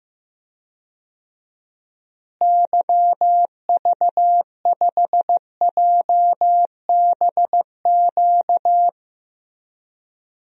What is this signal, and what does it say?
Morse code, heard as a keyed tone, about 15 wpm, 700 Hz. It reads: YV5JBQ
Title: DXCC-100 Call Signs - 6 Characters - Repeated 2x - 15wpm